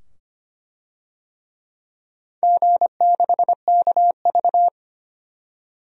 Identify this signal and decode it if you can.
Morse code, heard as a keyed tone, about 25 wpm, 700 Hz. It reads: G6KV
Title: DXCC-100 Call Signs - 4 Characters - 25wpm